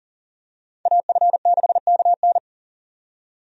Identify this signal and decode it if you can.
Morse code, heard as a keyed tone, about 40 wpm, 700 Hz. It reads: AF6KN
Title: USA Call Signs - 5 Characters - 40wpm